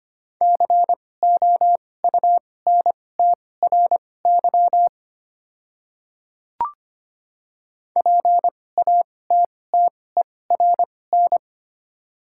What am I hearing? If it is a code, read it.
Morse code, heard as a keyed tone, about 25 wpm, spaced as for 18 wpm, 700 Hz. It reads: COUNTRY PATTERN
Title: 2x Character Spacing - Top 500 Words - 25wpm